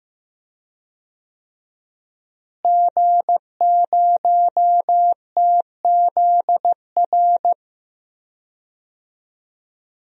Morse code, heard as a keyed tone, about 15 wpm, 700 Hz. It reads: G0TZR
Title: DXCC-100 Call Signs - 5 Characters - Repeated 2x - 15wpm